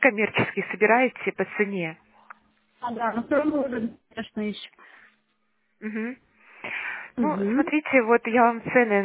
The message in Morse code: none

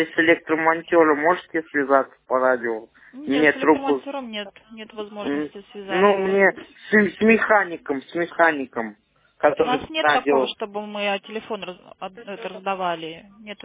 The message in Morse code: none

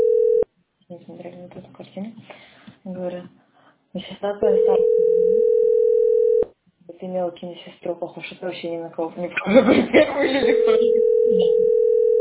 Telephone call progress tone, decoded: ringback tone

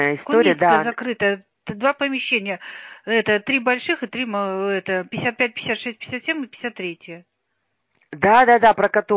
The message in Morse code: none

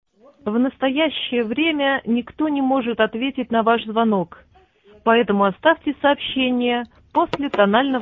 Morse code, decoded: none

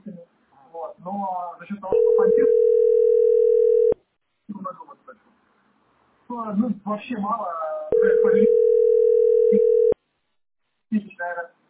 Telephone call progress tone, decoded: ringback tone

